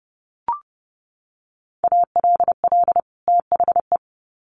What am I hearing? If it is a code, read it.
Morse code, heard as a keyed tone, about 30 wpm, 700 Hz. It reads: ALL THE
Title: Sets of 2 Words 30wpm